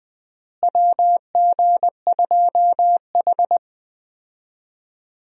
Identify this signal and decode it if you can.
Morse code, heard as a keyed tone, about 20 wpm, 700 Hz. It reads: WG2H
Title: USA Call Signs - 4 Characters - Repeated 2x - 20wpm